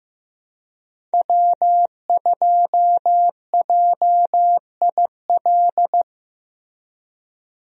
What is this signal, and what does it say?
Morse code, heard as a keyed tone, about 15 wpm, 700 Hz. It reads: W2JIL
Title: USA Call Signs - 5 Characters - Repeated 3x - 15wpm